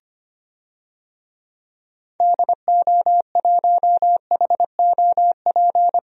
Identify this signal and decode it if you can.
Morse code, heard as a keyed tone, about 25 wpm, 700 Hz. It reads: DO1HOP
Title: DXCC-100 Call Signs - 6 Characters - 25wpm